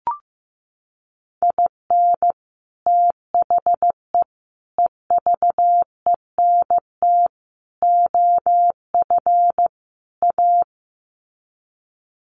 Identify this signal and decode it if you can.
Morse code, heard as a keyed tone, about 15 wpm, 700 Hz. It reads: IN THE EVENT OF A
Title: Sets of 5 Words Encore 15wpm